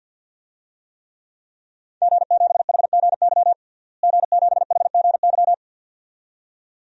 Morse code, heard as a keyed tone, about 47 wpm, 700 Hz. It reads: G7HGY G7HGY
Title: DXCC-100 Call Signs - 5 Characters - Repeated 2x - 50wpm